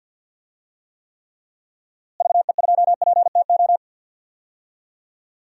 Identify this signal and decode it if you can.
Morse code, heard as a keyed tone, about 52 wpm, 700 Hz. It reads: VE2PTO
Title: DXCC-100 Call Signs - 6 Characters - Repeated 3x - 50wpm